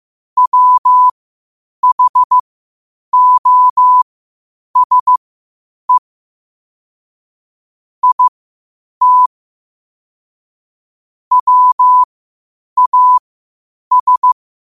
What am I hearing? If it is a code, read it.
Morse code, heard as a keyed tone, about 15 wpm, spaced as for 8 wpm, 1000 Hz. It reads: WHOSE IT WAS